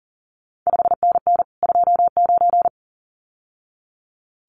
Morse code, heard as a keyed tone, about 38 wpm, 700 Hz. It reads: 5NN 29
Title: CQ WW DX Contest Exchange Encore - 40wpm